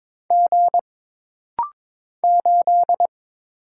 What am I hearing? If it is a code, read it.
Morse code, heard as a keyed tone, about 22 wpm, 700 Hz. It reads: G 8
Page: This rapid-fire timing is geared to push your ICR (Instant Character Recognition) skills.